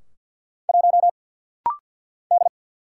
Morse code, heard as a keyed tone, about 50 wpm, 700 Hz. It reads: 1 D
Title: Single Letter-Number Alt - Rapid-Fire 50wpm